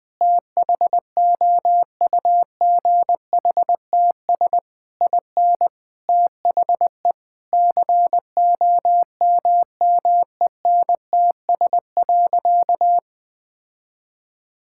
Morse code, heard as a keyed tone, about 20 wpm, 700 Hz. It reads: THOUGHTS IN THE COMMENTS.
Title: Sentences with 8-Word Phrases - 20wpm